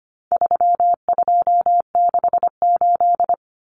Morse code, heard as a keyed tone, about 25 wpm, 700 Hz. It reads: 3268
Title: Numbers - 4-Digit 25wpm